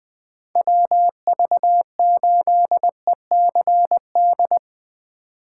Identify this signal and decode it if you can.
Morse code, heard as a keyed tone, about 21 wpm, 700 Hz. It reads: WV8ECD
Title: USA Call Signs - 2x Character Spacing - 20wpm